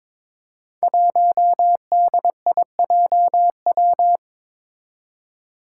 Morse code, heard as a keyed tone, about 22 wpm, 700 Hz. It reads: 1DIJW
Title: Five Challenging Letter-Number Combinations 22wpm